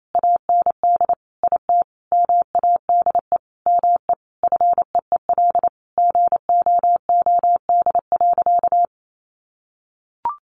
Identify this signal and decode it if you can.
Morse code, heard as a keyed tone, about 28 wpm, 700 Hz. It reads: AND IT MADE ME FEEL GOOD.
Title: Sentences from Top 600 Words 28wpm